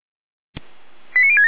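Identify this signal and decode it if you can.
Five-tone selective call (ZVEI), heard as quick runs of five tones, tones of about 70 ms. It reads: 89067
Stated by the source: ZVEI five-tone selective call received by SDR.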